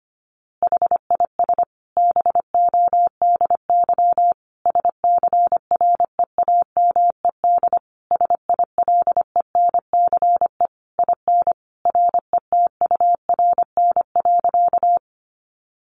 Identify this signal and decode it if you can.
Morse code, heard as a keyed tone, about 25 wpm, 700 Hz. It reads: HIS BODY SCREAMED SILENCE IN RETURN.
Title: A Sound Of Thunder 25wpm